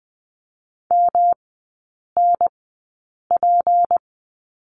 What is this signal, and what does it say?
Morse code, heard as a keyed tone, about 20 wpm, spaced as for 8 wpm, 700 Hz. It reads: MNP
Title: 152 Lesson 25e - Letter p - Three Characters 20wpm